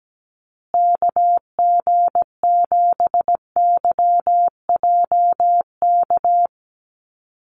Morse code, heard as a keyed tone, about 17 wpm, 700 Hz. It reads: KG7YJK
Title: USA Call Signs Repeated 3x 17wpm